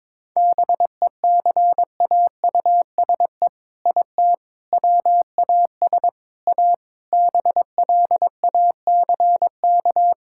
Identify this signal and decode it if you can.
Morse code, heard as a keyed tone, about 23 wpm, 700 Hz. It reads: BECAUSE IT WAS A BLACK
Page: morse-code-ninja-practice-22wpm